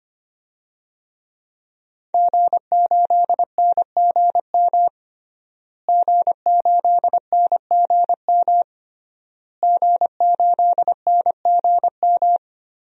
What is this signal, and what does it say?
Morse code, heard as a keyed tone, about 25 wpm, 700 Hz. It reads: G8NGM G8NGM G8NGM
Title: DXCC-100 Call Signs - 5 Characters - Repeated 3x - 25wpm